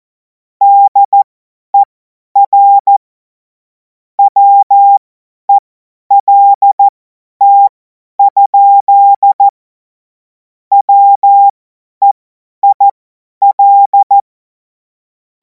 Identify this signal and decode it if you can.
Morse code, heard as a keyed tone, about 14 wpm, spaced as for 10 wpm, 800 Hz. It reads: DER WELT? WEIL
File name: Morsekurs / Witze / Witz 02. Juni 2026 50 F70